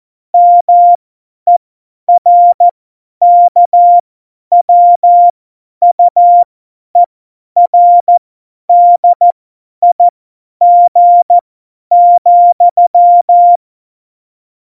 Morse code, heard as a keyed tone, about 14 wpm, spaced as for 10 wpm, 700 Hz. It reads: MERKWUERDIG,
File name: Morsekurs / Witze / Witz 05. September 17, Tempo 50